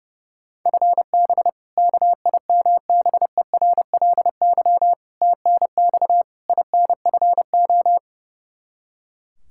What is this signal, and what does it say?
Morse code, heard as a keyed tone, about 32 wpm, 700 Hz. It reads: FB KIMBERLY TNX INFO
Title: Practice QSO - Exchange 3 - More Info - 30wpm